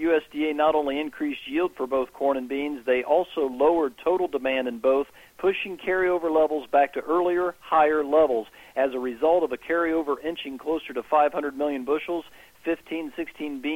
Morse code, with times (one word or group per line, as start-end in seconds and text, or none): none